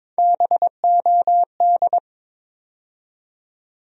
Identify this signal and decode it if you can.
Morse code, heard as a keyed tone, about 22 wpm, 700 Hz. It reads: BOD
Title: USA Call Sign Postfixes - 3 Letters 22wpm